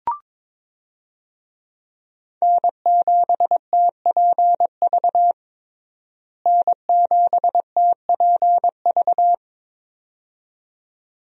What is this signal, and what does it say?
Morse code, heard as a keyed tone, about 22 wpm, 700 Hz. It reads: N7TPV N7TPV
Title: USA Call Signs - 5 Characters - Repeated 2x - 22wpm